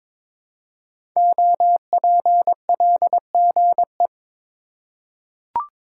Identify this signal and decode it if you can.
Morse code, heard as a keyed tone, about 22 wpm, 700 Hz. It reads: OPLGE
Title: Code Groups - 5 Letters - 22wpm